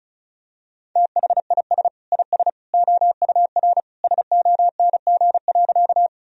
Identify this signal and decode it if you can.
Morse code, heard as a keyed tone, about 36 wpm, 700 Hz. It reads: THIS IS OUR SONG.
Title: Sentences from Top 1000 Words - Review Part 2 - 35wpm